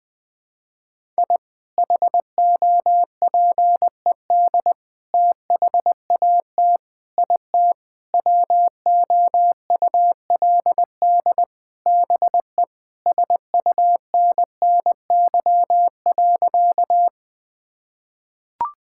Morse code, heard as a keyed tone, about 21 wpm, 700 Hz. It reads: I HOPED THAT IT WOULD BE SUNNY.
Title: Sentences with 4-Word Phrases - 20wpm